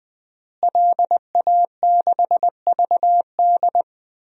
Morse code, heard as a keyed tone, about 21 wpm, 700 Hz. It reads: LA6VD
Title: DXCC-100 Call Signs - 5 Characters - Repeated 3x - 20wpm